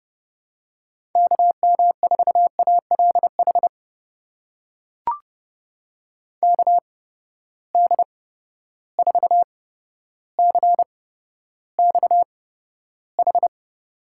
Morse code, heard as a keyed tone, about 32 wpm, spaced as for 13 wpm, 700 Hz. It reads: KM4ALH KD4CXH
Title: USA Call Signs - 6 Characters - 8x Character Spacing - 30wpm